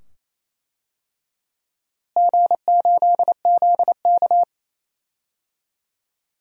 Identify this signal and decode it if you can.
Morse code, heard as a keyed tone, about 28 wpm, 700 Hz. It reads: G8ZK
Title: DXCC-100 Call Signs - 4 Characters - Repeated 2x - 28wpm